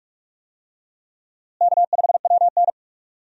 Morse code, heard as a keyed tone, about 42 wpm, 700 Hz. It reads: K5WN
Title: USA Call Signs - 4 Characters - Repeated 3x - 45wpm